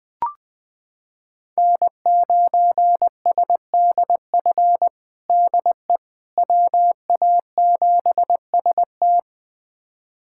Morse code, heard as a keyed tone, about 20 wpm, 700 Hz. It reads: N9SDF DE WA7ST
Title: USA Callsign DE Callsign - 5 Characters - 20wpm